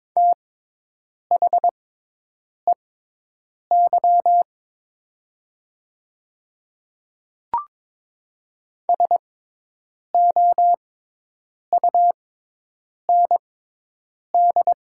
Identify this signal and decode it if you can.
Morse code, heard as a keyed tone, about 22 wpm, spaced as for 8 wpm, 700 Hz. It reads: THEY SOUND
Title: 6x Character Spacing - Top 100 Words - 22wpm